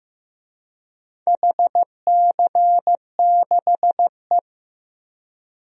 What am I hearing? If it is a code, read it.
Morse code, heard as a keyed tone, about 15 wpm, 700 Hz. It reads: HC6E
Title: DXCC-100 Call Signs - 4 Characters - Repeated 3x - 15wpm